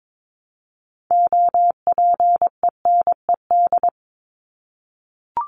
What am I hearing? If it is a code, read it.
Morse code, heard as a keyed tone, about 22 wpm, 700 Hz. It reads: OPENED
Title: ED Suffix - Top 100 Words with Base Words 22wpm